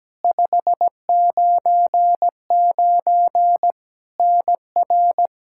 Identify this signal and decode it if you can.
Morse code, heard as a keyed tone, about 17 wpm, 700 Hz. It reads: 599 NR
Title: Practice QSO - Exchange 1 - Beginning - 17wpm